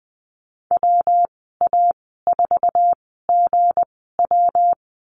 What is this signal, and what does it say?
Morse code, heard as a keyed tone, about 20 wpm, spaced as for 14 wpm, 700 Hz. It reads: WA4GW
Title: 179 Lesson 29g - Slash - Call Signs 20wpm